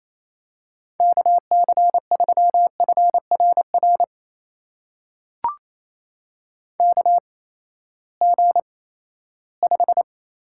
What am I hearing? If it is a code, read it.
Morse code, heard as a keyed tone, about 27 wpm, spaced as for 16 wpm, 700 Hz. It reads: KC3FRR KG5
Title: USA Call Signs - 6 Characters - 8x Character Spacing - 28wpm